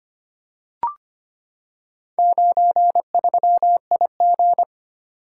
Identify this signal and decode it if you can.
Morse code, heard as a keyed tone, about 25 wpm, 700 Hz. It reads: 93IG
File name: practice / 25wpm / Four Letter-Number Combinations - 25wpm